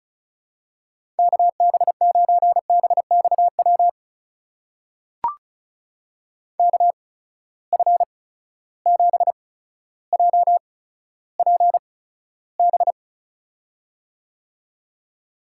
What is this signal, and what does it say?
Morse code, heard as a keyed tone, about 36 wpm, spaced as for 15 wpm, 700 Hz. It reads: KB9BXW KF7JPB
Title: USA Call Signs - 6 Characters - 8x Character Spacing - 35wpm